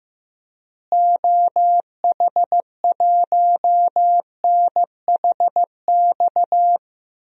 Morse code, heard as a keyed tone, about 15 wpm, 700 Hz. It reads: OH1NHX
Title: DXCC-100 Call Signs - 6 Characters - Repeated 3x - 15wpm